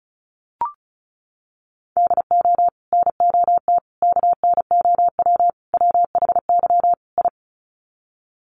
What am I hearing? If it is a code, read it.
Morse code, heard as a keyed tone, about 34 wpm, 700 Hz. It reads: DO NOT KNOW WHY I
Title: Sets of 5 Words Encore 35wpm